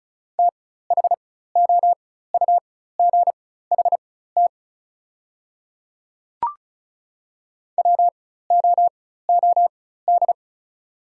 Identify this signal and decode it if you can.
Morse code, heard as a keyed tone, about 34 wpm, spaced as for 16 wpm, 700 Hz. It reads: THOUGHT WOOD
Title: 4x Character Spacing - Top 300 Words - 35wpm